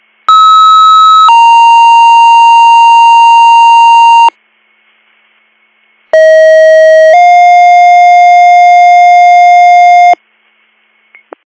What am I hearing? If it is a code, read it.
Two-tone page, heard as a one-second tone, then a three-second tone, about 1290 then 930 Hz, about 640 then 710 Hz. It is heard from 0.3 to 10.1 s.